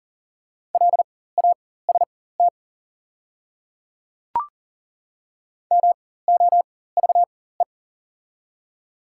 Morse code, heard as a keyed tone, about 40 wpm, spaced as for 19 wpm, 690 Hz. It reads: LAST MOVE
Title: 4x Character Spacing - Top 300 Words - 40wpm